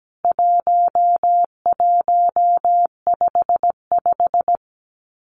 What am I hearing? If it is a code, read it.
Morse code, heard as a keyed tone, about 17 wpm, 700 Hz. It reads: 1155
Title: CWT Exchange 17wpm